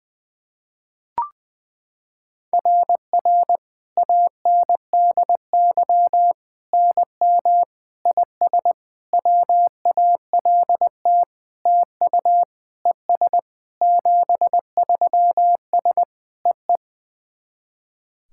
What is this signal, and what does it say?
Morse code, heard as a keyed tone, about 20 wpm, 690 Hz. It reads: RR ANDY NM IS WALT TU ES 73S EE
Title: SOTA QSO Practice 20wpm